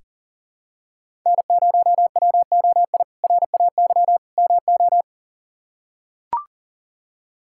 Morse code, heard as a keyed tone, about 40 wpm, 700 Hz. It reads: N0WOI RAY MO
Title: K1USN SST QSO Practice - 40wpm